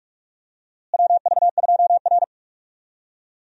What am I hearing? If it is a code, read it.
Morse code, heard as a keyed tone, about 47 wpm, 700 Hz. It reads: WV2R
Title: USA Call Signs - 4 Characters - Repeated 2x - 45wpm